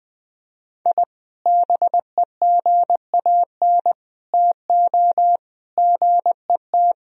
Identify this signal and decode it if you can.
Morse code, heard as a keyed tone, about 20 wpm, 700 Hz. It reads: I BEGAN TO GET
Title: Sentences from Top 1000 Words - Review Part 4 - 20wpm